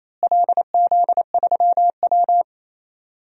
Morse code, heard as a keyed tone, about 28 wpm, 700 Hz. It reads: LZ3W
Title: DXCC-100 Call Signs - 4 Characters - Repeated 3x - 28wpm